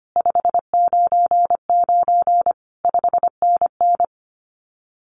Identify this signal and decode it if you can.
Morse code, heard as a keyed tone, about 25 wpm, 700 Hz. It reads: 599 5NN
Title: Common QSO Phrases Encore 25wpm